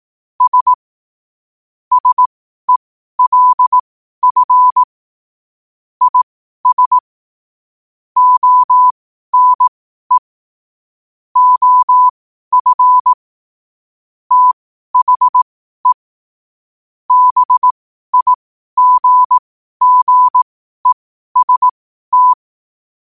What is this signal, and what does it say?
Morse code, heard as a keyed tone, about 18 wpm, spaced as for 12 wpm, 1000 Hz. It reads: S SELF IS ONE OF THE BIGGEST
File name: qotdfiles / 12wpm